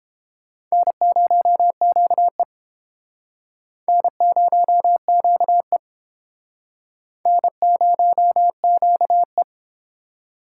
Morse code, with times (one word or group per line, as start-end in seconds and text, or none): 0.7-2.4 N0QE
3.9-5.8 N0QE
7.3-9.4 N0QE